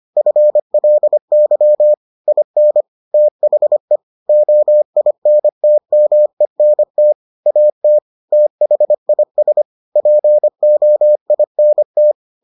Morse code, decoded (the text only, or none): FLY IN THE OINTMENT AT THIS POINT